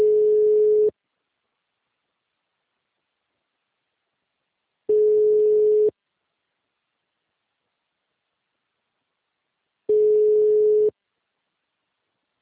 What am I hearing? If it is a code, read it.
Ringback tone (Europe, 425 Hz).
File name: out- -2023- - - .63346